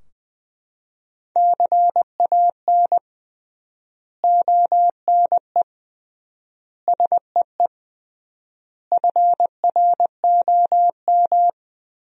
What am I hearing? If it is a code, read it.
Morse code, heard as a keyed tone, about 20 wpm, 700 Hz. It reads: CAN ONE SEE FROM
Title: 3x Word-Spacing - Sentences from Top 100 Words - 20wpm